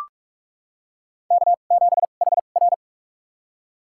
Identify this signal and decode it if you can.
Morse code, heard as a keyed tone, about 46 wpm, spaced as for 32 wpm, 690 Hz. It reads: K7HR
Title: USA Call Signs - 4 Characters - 2x Character Spacing - 45wpm